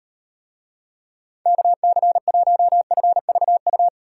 Morse code, heard as a keyed tone, about 38 wpm, 700 Hz. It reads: KC1FVU